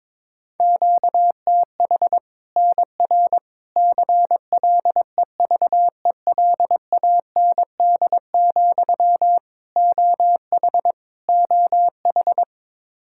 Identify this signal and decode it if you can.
Morse code, heard as a keyed tone, about 22 wpm, 700 Hz. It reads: QTH NR CLEVELAND, OH OH
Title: K1USN SST QSO Practice 22wpm